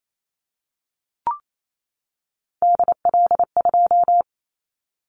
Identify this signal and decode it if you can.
Morse code, heard as a keyed tone, about 28 wpm, 700 Hz. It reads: DL2
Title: Callsign Prefixes for Top 25 DXCC - 28wpm